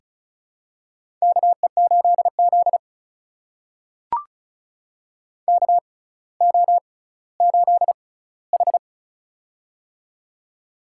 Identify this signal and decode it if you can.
Morse code, heard as a keyed tone, about 34 wpm, spaced as for 18 wpm, 700 Hz. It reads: KE8Z KO8H
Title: USA Call Signs - 4 Characters - 6x Character Spacing - 35wpm